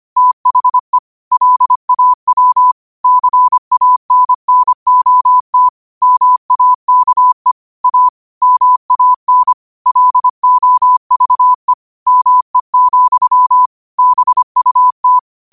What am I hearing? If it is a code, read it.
Morse code, heard as a keyed tone, about 25 wpm, 1000 Hz. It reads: THE LAW CANNOT MAKE A MAN LOVE ME, BUT
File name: qotdfiles / 25wpm